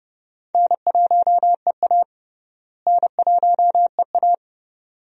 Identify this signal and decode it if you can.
Morse code, heard as a keyed tone, about 31 wpm, 700 Hz. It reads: N1EA N1EA
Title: USA Call Signs - 4 Characters - Repeated 2x - 30wpm